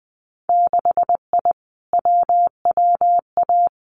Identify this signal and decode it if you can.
Morse code, heard as a keyed tone, about 20 wpm, 700 Hz. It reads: 6I WWA
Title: Winter Field Day Exchanges Encore 20wpm